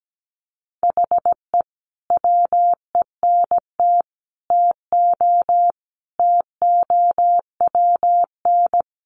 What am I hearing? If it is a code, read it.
Morse code, heard as a keyed tone, about 17 wpm, 700 Hz. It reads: HE WENT TO TOWN
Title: Sentences from Top 1300 Words 17wpm